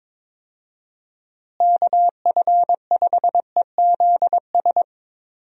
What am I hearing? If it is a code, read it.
Morse code, heard as a keyed tone, about 22 wpm, 700 Hz. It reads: KF5EZS
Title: USA Call Signs - 6 Characters - Repeated 3x - 22wpm